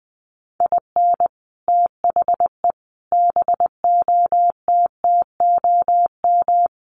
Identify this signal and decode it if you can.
Morse code, heard as a keyed tone, about 20 wpm, 700 Hz. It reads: IN THE BOTTOM
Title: Sets of 5 Words Encore 20wpm